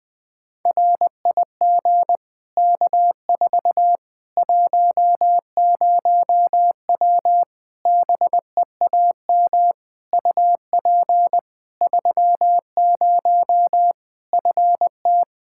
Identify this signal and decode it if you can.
Morse code, heard as a keyed tone, about 20 wpm, 700 Hz. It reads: RIG K4 10W BEAM UP 30 FT
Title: K1USN SST QSO Practice 20wpm